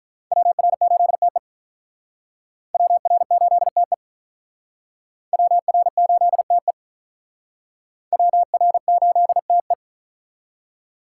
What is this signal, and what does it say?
Morse code, heard as a keyed tone, about 45 wpm, 700 Hz. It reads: WR8TE WR8TE WR8TE WR8TE